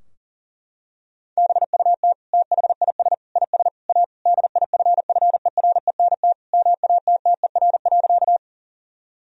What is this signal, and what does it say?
Morse code, heard as a keyed tone, about 40 wpm, 700 Hz. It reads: BUT THIS IS A DIFFERENT MATTER.
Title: Sentences from Top 1000 Words - Review Part 3 - 40wpm